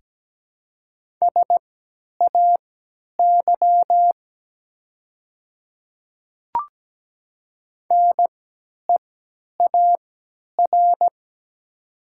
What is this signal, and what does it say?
Morse code, heard as a keyed tone, about 17 wpm, spaced as for 10 wpm, 700 Hz. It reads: SAY NEAR
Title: 3x Character Spacing - Top 300 Words - 17wpm